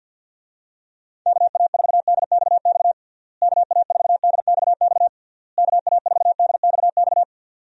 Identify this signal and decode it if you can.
Morse code, heard as a keyed tone, about 51 wpm, 700 Hz. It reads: KA4DXX KA4DXX KA4DXX
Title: USA Call Signs - 6 Characters - Repeated 3x - 50wpm